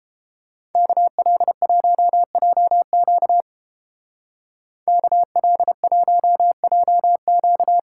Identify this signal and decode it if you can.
Morse code, heard as a keyed tone, about 30 wpm, 700 Hz. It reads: KL1JQ KL1JQ